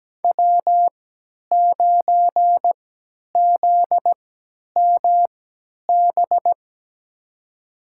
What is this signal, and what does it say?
Morse code, heard as a keyed tone, about 17 wpm, spaced as for 10 wpm, 700 Hz. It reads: W9ZMB
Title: USA Call Signs - 5 Characters - 3x Character Spacing - 17wpm